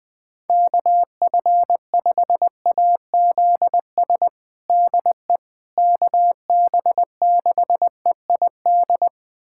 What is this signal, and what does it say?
Morse code, heard as a keyed tone, about 21 wpm, 700 Hz. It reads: KF5AZS DE KB6EID
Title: USA Callsign DE Callsign - 6 Characters - 20wpm